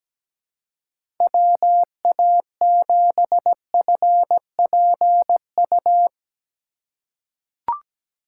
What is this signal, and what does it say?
Morse code, heard as a keyed tone, about 17 wpm, 700 Hz. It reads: WA7FPU
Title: USA Call Signs - 6 Characters - 2x Character Spacing - 17wpm